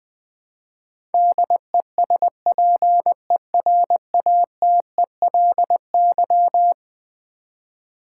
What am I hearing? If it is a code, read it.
Morse code, heard as a keyed tone, about 20 wpm, 700 Hz. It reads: DESPERATELY
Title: Eleven Letter Words - 20wpm